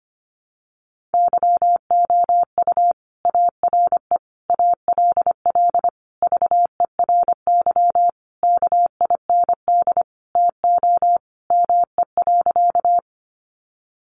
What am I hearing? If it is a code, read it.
Morse code, heard as a keyed tone, about 25 wpm, 700 Hz. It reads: YOU ARE ALL VERY KIND TO ME.
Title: Sentences from Top 200 Words 25wpm